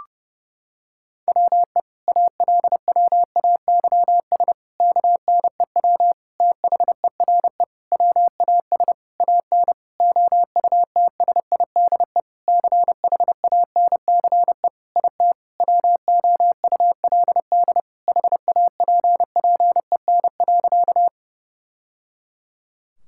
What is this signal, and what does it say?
Morse code, heard as a keyed tone, about 32 wpm, 700 Hz. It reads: WE ALWAYS KNEW THERE WAS AN OUTSIDE CHANCE IT WOULD HAPPEN.
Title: Sentences from Top 1000 Words - Review Part 6 - 30wpm